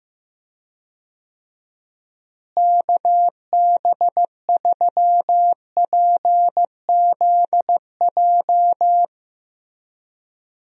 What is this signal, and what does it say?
Morse code, heard as a keyed tone, about 15 wpm, 700 Hz. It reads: KB3PZJ
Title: USA Call Signs - 6 Characters - Repeated 2x - 15wpm